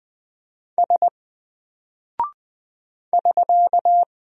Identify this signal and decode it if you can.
Morse code, heard as a keyed tone, about 20 wpm, 700 Hz. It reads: S <SK>
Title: Single Character (Letter:Number:Punctuation:Prosign) Alt 20wpm